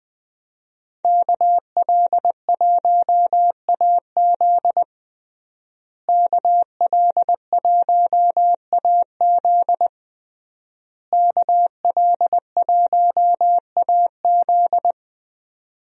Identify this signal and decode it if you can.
Morse code, heard as a keyed tone, about 21 wpm, 700 Hz. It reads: KL1AZ KL1AZ KL1AZ
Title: USA Call Signs - 5 Characters - Repeated 3x - 20wpm